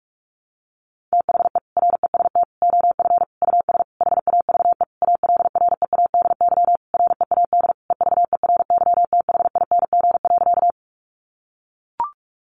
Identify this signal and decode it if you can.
Morse code, heard as a keyed tone, about 44 wpm, 700 Hz. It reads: THE REST OF US HAVE ALREADY READ EVERYTHING.
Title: Sentences from Top 700 Words 45wpm